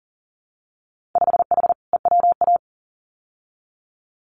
Morse code, heard as a keyed tone, about 42 wpm, 700 Hz. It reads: 5H EWA
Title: Winter Field Day Exchanges Encore 40wpm